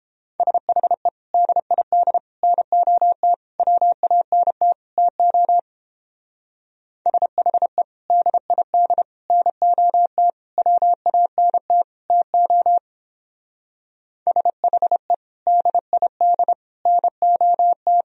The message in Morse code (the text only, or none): SHE DID NOT WANT TO SHE DID NOT WANT TO SHE DID NOT